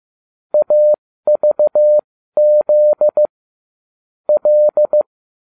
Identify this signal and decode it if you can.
Morse code, heard as a keyed tone, about 15 wpm, spaced as for 13 wpm, 600 Hz. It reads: AVZ L